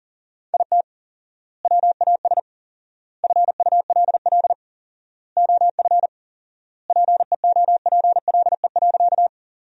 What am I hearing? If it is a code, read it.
Morse code, heard as a keyed tone, about 37 wpm, 700 Hz. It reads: IT WAS FULL OF PEOPLE.
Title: 4x Word-Spacing - Sentences from Top 500 Words - 40wpm